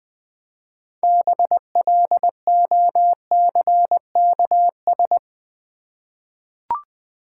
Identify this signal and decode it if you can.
Morse code, heard as a keyed tone, about 20 wpm, 690 Hz. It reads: BLOCKS
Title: Easy Words with Hints - 20wpm